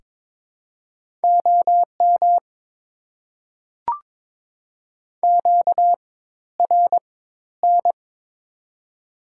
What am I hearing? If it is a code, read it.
Morse code, heard as a keyed tone, about 22 wpm, spaced as for 13 wpm, 700 Hz. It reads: OM QRN
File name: Top 90 QSO Elements - 4x Character Spacing - 22wpm